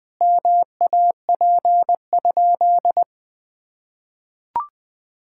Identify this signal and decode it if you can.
Morse code, heard as a keyed tone, about 20 wpm, 700 Hz. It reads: MAP?
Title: Sentences from Top 1000 Words - Review Part 5 - 20wpm